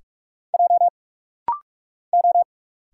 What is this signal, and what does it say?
Morse code, heard as a keyed tone, about 45 wpm, 690 Hz. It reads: J O
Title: Single Letters - Rapid-Fire 45wpm